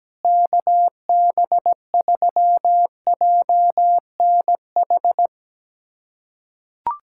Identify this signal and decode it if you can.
Morse code, heard as a keyed tone, about 17 wpm, 700 Hz. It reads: KB3JNH
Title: USA Callsign DE Callsign - 6 Characters - 17wpm